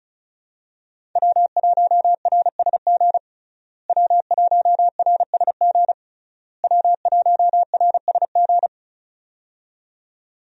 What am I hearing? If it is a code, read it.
Morse code, heard as a keyed tone, about 36 wpm, 700 Hz. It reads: W1RSG W1RSG W1RSG
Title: USA Call Signs - 5 Characters - Repeated 3x - 35wpm